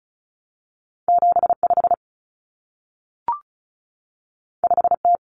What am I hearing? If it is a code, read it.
Morse code, heard as a keyed tone, about 36 wpm, 700 Hz. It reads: 75 5T
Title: Common Cut Numbers - 2-Digit 35wpm